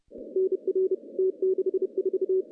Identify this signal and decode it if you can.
Morse code, heard as a keyed tone, about 30 wpm, 380 Hz. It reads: NR T64
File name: CW-SS